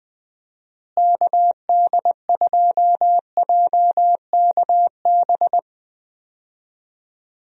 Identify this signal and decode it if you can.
Morse code, heard as a keyed tone, about 21 wpm, 700 Hz. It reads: KD2JKB
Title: USA Call Signs - 6 Characters - Repeated 2x - 20wpm